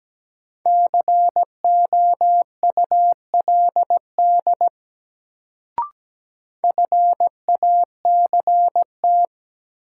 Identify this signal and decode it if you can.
Morse code, heard as a keyed tone, about 17 wpm, 700 Hz. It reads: COULD FACT
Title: Sets of 2 Words - New Word Drill - 17wpm